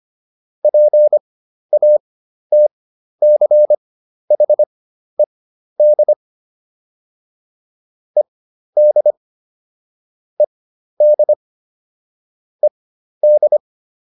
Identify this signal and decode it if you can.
Morse code, heard as a keyed tone, about 25 wpm, spaced as for 12 wpm, 600 Hz. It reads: PATCHED ED ED ED